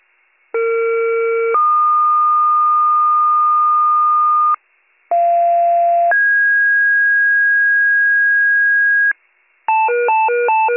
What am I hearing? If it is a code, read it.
Two-tone page, heard as a one-second tone, then a three-second tone, about 460 then 1220 Hz, about 690 then 1650 Hz, with noise. It is heard from 0.5 to 9.1 s.